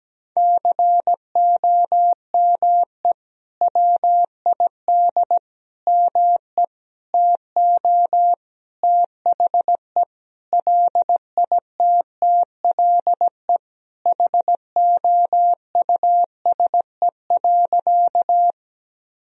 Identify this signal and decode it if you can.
Morse code, heard as a keyed tone, about 17 wpm, 700 Hz. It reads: COME WID ME TO THE LITTLE HOUSE.
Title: Sentences from Top 200 Words and Abbreviations 17wpm